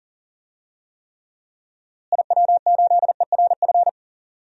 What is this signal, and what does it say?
Morse code, heard as a keyed tone, about 39 wpm, 700 Hz. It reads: IW8ERF
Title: DXCC-100 Call Signs - 6 Characters - Repeated 3x - 40wpm